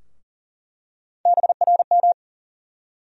Morse code, heard as a keyed tone, about 40 wpm, 700 Hz. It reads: BRM
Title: USA Call Sign Postfixes - 3 Letters 40wpm